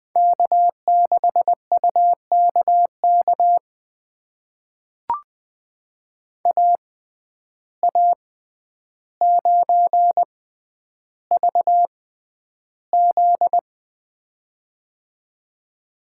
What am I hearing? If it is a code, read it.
Morse code, heard as a keyed tone, about 20 wpm, spaced as for 10 wpm, 700 Hz. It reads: K6UKK AA9VZ
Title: USA Call Signs - 5 Characters - 6x Character Spacing - 20wpm